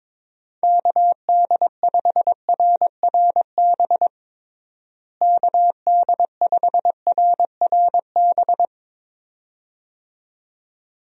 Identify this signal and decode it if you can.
Morse code, heard as a keyed tone, about 22 wpm, 700 Hz. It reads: KD5RRB KD5RRB
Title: USA Call Signs - 6 Characters - Repeated 2x - 22wpm